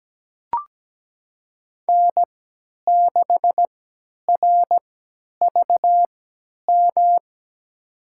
Morse code, heard as a keyed tone, about 17 wpm, spaced as for 10 wpm, 700 Hz. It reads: N6RVM
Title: USA Call Signs - 5 Characters - 3x Character Spacing - 17wpm